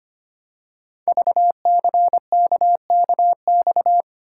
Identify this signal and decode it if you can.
Morse code, heard as a keyed tone, about 25 wpm, 700 Hz. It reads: VCKKX
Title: Code Groups - 5 Letters - 25wpm